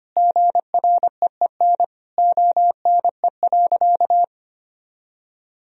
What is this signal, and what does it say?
Morse code, heard as a keyed tone, about 25 wpm, 700 Hz. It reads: GREEN ONE.
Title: Sentences from Top 1000 Words - Review Part 2 - 25wpm